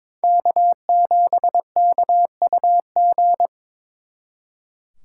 Morse code, heard as a keyed tone, about 26 wpm, 700 Hz. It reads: K7KUG
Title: Speed Racing - USA 1x3 Call Signs 22wpm